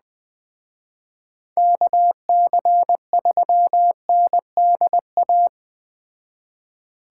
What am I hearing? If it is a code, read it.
Morse code, heard as a keyed tone, about 20 wpm, 700 Hz. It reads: KC3NDA